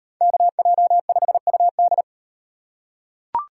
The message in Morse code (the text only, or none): KJ5UD